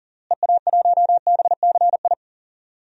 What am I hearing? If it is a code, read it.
Morse code, heard as a keyed tone, about 39 wpm, 700 Hz. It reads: EA1BCI
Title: DXCC-100 Call Signs - 6 Characters - Repeated 3x - 40wpm